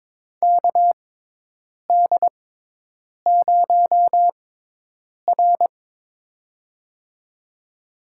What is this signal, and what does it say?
Morse code, heard as a keyed tone, about 22 wpm, spaced as for 8 wpm, 700 Hz. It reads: KD0R
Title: USA Call Signs - 4 Characters - 6x Character Spacing - 22wpm